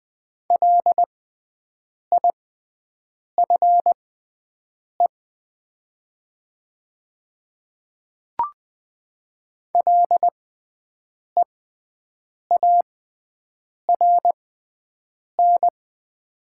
Morse code, heard as a keyed tone, about 20 wpm, spaced as for 7 wpm, 700 Hz. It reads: LIFE LEARN
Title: 6x Character Spacing - Top 300 Words - 20wpm